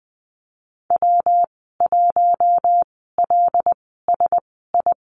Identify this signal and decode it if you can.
Morse code, heard as a keyed tone, about 20 wpm, spaced as for 15 wpm, 700 Hz. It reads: W1LSI
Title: 122 Lesson 20g - Number 3 - Call Signs 20wpm